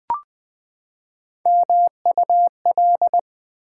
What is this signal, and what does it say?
Morse code, heard as a keyed tone, about 20 wpm, 700 Hz. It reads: MUL
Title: Build A Word - 3 Letters At A Time - 8-Letter Words - 20wpm